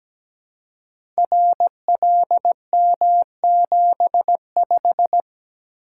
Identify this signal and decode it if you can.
Morse code, heard as a keyed tone, about 17 wpm, 700 Hz. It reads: RLM75
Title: Code Groups - 5 Letter:Numbers - 17wpm